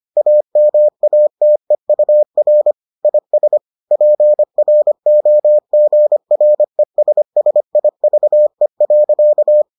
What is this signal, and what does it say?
Morse code, heard as a keyed tone, about 25 wpm, 600 Hz. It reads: AMATEUR IS PROGRESSIVE.